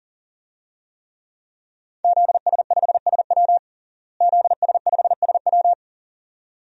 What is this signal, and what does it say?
Morse code, heard as a keyed tone, about 40 wpm, 700 Hz. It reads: ZS5SW ZS5SW
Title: DXCC-100 Call Signs - 5 Characters - Repeated 2x - 40wpm